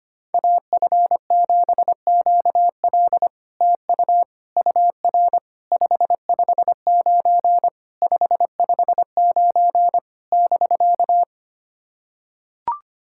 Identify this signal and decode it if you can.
Morse code, heard as a keyed tone, about 26 wpm, 700 Hz. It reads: AF7QL TU UR 559 559 <BK>
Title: SOTA QSO Practice 25wpm